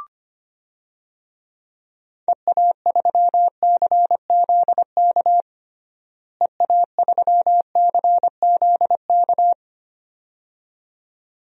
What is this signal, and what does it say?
Morse code, heard as a keyed tone, about 25 wpm, 700 Hz. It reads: EA3CZK EA3CZK
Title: DXCC-100 Call Signs - 6 Characters - Repeated 2x - 25wpm